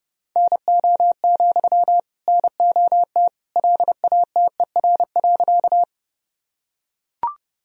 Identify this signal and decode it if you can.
Morse code, heard as a keyed tone, about 32 wpm, 700 Hz. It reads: NO, NOT LATER.
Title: Sentences from Top 1000 Words - Review Part 4 - 30wpm